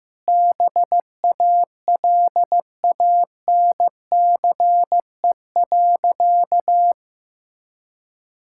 Morse code, heard as a keyed tone, about 15 wpm, 700 Hz. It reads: BALANCE.
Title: Sentences from Top 1000 Words - Review Part 6 - 15wpm